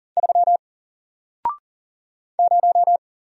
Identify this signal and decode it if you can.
Morse code, heard as a keyed tone, about 40 wpm, 700 Hz. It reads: 3 0
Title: Single Letter-Number Alt 40wpm